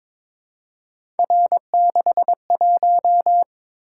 Morse code, heard as a keyed tone, about 22 wpm, 700 Hz. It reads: R61
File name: practice / 22wpm / Three Letter-Number Combinations - 22wpm